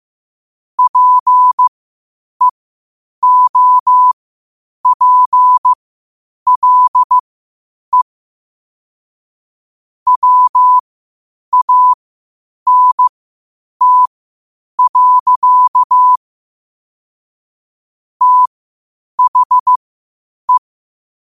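Morse code, heard as a keyed tone, about 15 wpm, spaced as for 8 wpm, 1000 Hz. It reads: PEOPLE WANT. THE
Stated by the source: Morse code at 8 wpm.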